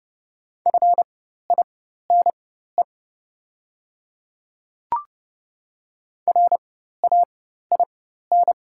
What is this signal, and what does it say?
Morse code, heard as a keyed tone, about 30 wpm, spaced as for 14 wpm, 690 Hz. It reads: FINE RAIN
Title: 4x Character Spacing - Top 500 Words - 30wpm